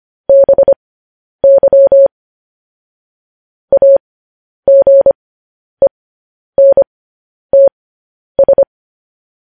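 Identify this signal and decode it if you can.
Morse code, heard as a keyed tone, about 25 wpm, spaced as for 10 wpm, 550 Hz. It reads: BY AGENTS
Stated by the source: Morse code news headlines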